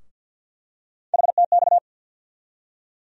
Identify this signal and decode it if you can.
Morse code, heard as a keyed tone, about 52 wpm, 700 Hz. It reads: HTX
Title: USA Call Sign Postfixes - 3 Letters 50wpm